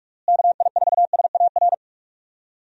Morse code, heard as a keyed tone, about 41 wpm, 700 Hz. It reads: KI4SAR
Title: USA Call Signs - 6 Characters - Repeated 3x - 45wpm